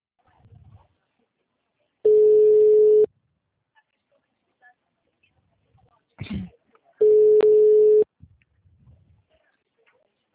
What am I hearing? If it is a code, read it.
Ringback tone (Europe, 425 Hz).
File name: out- -2013- - - .92395